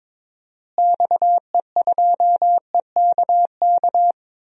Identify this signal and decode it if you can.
Morse code, heard as a keyed tone, about 22 wpm, 700 Hz. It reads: XE2EKK
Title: DXCC-100 Call Signs - 6 Characters - Repeated 2x - 22wpm